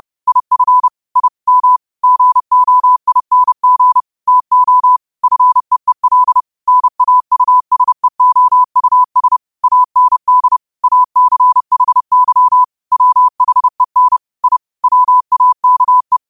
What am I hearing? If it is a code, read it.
Morse code, heard as a keyed tone, about 30 wpm, 1000 Hz. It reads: IF I M GOING TO FEEL NAUSEOUS AND ACHY WHEN I WAKE